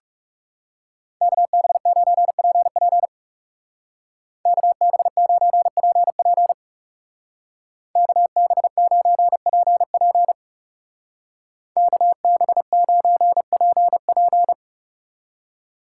Morse code, heard as a keyed tone, about 35 wpm, 700 Hz. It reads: KB9PP KB9PP KB9PP KB9PP
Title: Speed Racing - USA 2x2 Call Signs 30wpm